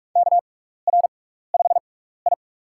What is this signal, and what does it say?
Morse code, heard as a keyed tone, about 46 wpm, spaced as for 16 wpm, 700 Hz. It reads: KR5I
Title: USA Call Signs - 4 Characters - 6x Character Spacing - 45wpm